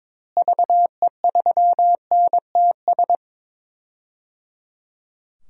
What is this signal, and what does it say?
Morse code, heard as a keyed tone, about 22 wpm, 700 Hz. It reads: VE3NTS
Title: DXCC-100 Call Signs - 6 Characters - Repeated 3x - 22wpm